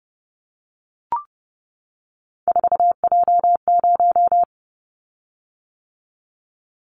Morse code, heard as a keyed tone, about 29 wpm, 700 Hz. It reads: 4J0
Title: Callsign Prefixes for Top 100 DXCC - 30wpm